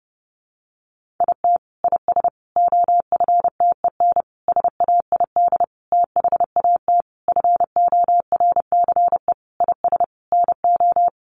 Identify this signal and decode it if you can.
Morse code, heard as a keyed tone, about 30 wpm, 700 Hz. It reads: IT IS OFTEN SAID THAT FORCE IS NO
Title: Sentences from Top 1400 Words 30wpm